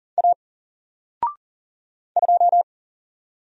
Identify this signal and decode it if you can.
Morse code, heard as a keyed tone, about 40 wpm, 700 Hz. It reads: A 2
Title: Single Character (Letter:Number:Punctuation:Prosign) Alt 40wpm